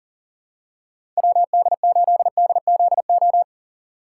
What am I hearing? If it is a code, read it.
Morse code, heard as a keyed tone, about 40 wpm, 700 Hz. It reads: WD8DZO
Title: USA Call Signs - 6 Characters - 6x Character Spacing - 40wpm